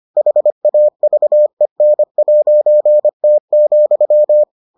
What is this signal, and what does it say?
Morse code, heard as a keyed tone, about 25 wpm, 600 Hz. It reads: HAVEN'T,